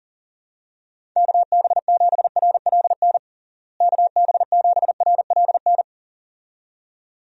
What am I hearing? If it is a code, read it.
Morse code, heard as a keyed tone, about 40 wpm, 700 Hz. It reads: KB7RLN KB7RLN
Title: USA Call Signs - 6 Characters - Repeated 2x - 40wpm